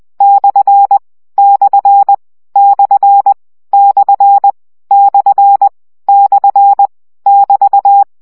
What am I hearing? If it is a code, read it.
Morse code, heard as a keyed tone, about 20 wpm, 800 Hz. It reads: / / / / / / <BT>
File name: CW gyakorlas per 17 baud